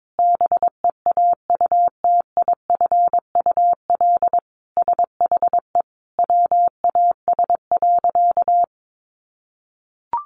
Morse code, heard as a keyed tone, about 22 wpm, 700 Hz. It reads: BEAUTIFUL SHE WAS.
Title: Sentences from Top 900 Words 22wpm